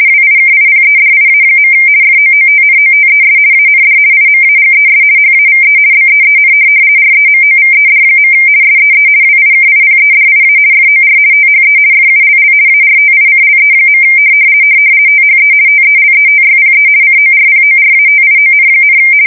RTTY, about 50 baud. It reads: WIKIPEDIA. FOOTNOTES AND IMAGES HAVE BEEN REMOVED. SEE WIKIPEDIA FOR THE FULL ARTICLE. THE TELETYPE CORPORATION,
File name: fsk170-50B/teletype-corp